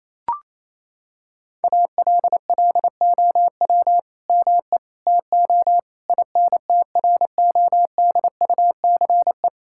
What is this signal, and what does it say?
Morse code, heard as a keyed tone, about 28 wpm, 700 Hz. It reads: ALLOW ME TO INTRODUCE
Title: Sentences from Top 1600 Words 28wpm